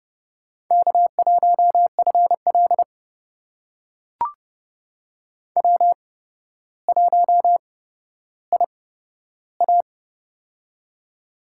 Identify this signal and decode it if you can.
Morse code, heard as a keyed tone, about 29 wpm, spaced as for 13 wpm, 700 Hz. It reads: K1FL W1IA
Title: USA Call Signs - 4 Characters - 8x Character Spacing - 30wpm